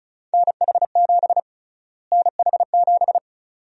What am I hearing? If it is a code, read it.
Morse code, heard as a keyed tone, about 36 wpm, 700 Hz. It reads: NH7 NH7
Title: USA Call Sign Prefixes - Repeated 2x - 35wpm